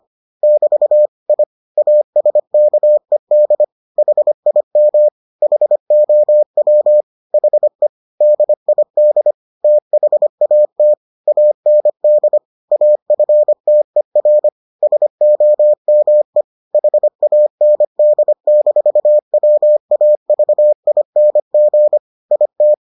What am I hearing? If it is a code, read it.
Morse code, heard as a keyed tone, about 25 wpm, 600 Hz. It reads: <BT> I ASKED HIM HOW HE DID THAT AND AFTER SOME HAND-WAVING IT